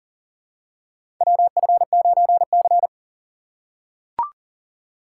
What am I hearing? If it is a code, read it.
Morse code, heard as a keyed tone, about 40 wpm, 700 Hz. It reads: WF9C
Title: USA Call Signs - 4 Characters - 2x Character Spacing - 40wpm